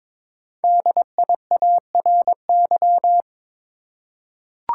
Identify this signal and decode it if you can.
Morse code, heard as a keyed tone, about 22 wpm, 700 Hz. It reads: DIARY
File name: 5 Letter Words - First 3 Letters Given - 22wpm